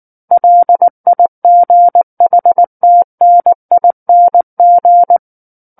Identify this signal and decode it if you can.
Morse code, heard as a keyed tone, about 19 wpm, 700 Hz. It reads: LIGHTNING
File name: Book One / 10-Chapter 10 - 19 WPM